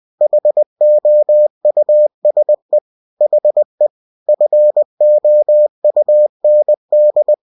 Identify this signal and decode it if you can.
Morse code, heard as a keyed tone, about 20 wpm, 600 Hz. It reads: HOUSE HE FOUND